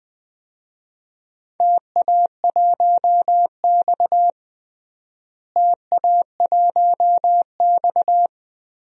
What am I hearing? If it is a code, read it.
Morse code, heard as a keyed tone, about 20 wpm, 700 Hz. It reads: TA1X TA1X
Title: DXCC-100 Call Signs - 4 Characters - Repeated 3x - 20wpm